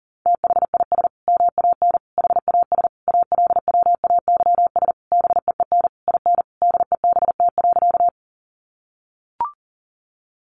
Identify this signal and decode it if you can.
Morse code, heard as a keyed tone, about 40 wpm, 700 Hz. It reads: THIS MAN HAS ALWAYS BEEN IN DEBT.
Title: Sentences from Top 1300 Words 40wpm